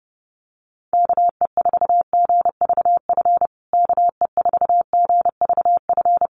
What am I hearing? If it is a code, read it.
Morse code, heard as a keyed tone, about 30 wpm, 700 Hz. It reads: KE4GVF KE4GVF
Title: USA Call Signs Repeated 3x 30wpm